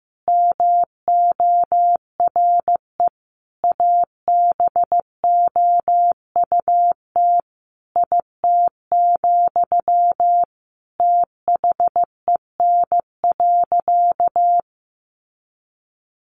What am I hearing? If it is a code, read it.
Morse code, heard as a keyed tone, about 15 wpm, 700 Hz. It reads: MORE ABOUT IT, THEN.
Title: Sentences from Top 200 Words 15wpm